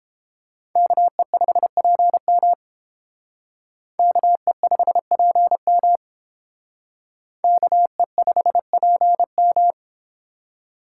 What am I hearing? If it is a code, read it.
Morse code, heard as a keyed tone, about 30 wpm, 700 Hz. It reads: KE5PM KE5PM KE5PM